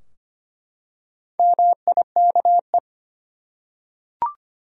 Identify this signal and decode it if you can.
Morse code, heard as a keyed tone, about 25 wpm, 700 Hz. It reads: MIKE
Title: USA Common Names - Max 4 Letters - 25wpm